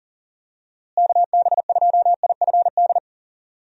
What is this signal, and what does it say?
Morse code, heard as a keyed tone, about 40 wpm, 700 Hz. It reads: KB2IFD
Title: USA Call Signs - 6 Characters - Repeated 2x - 40wpm